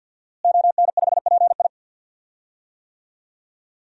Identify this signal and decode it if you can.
Morse code, heard as a keyed tone, about 52 wpm, 700 Hz. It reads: ON5PI
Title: DXCC-100 Call Signs - 5 Characters - Repeated 3x - 50wpm